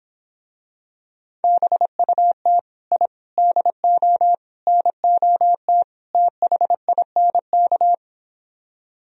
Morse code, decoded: BUT I DO NOT THINK

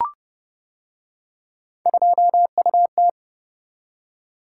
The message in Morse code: 2UT

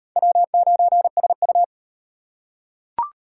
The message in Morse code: W9SU

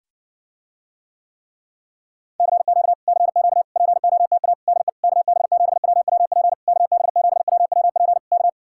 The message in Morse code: CQ CQ POTA DE KB7WWP KB7WWP K

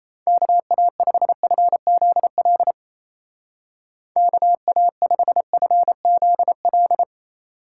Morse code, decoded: KA5FZL KA5FZL

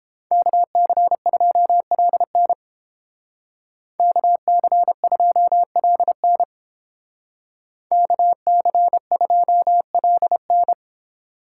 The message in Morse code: KC2LN KC2LN KC2LN

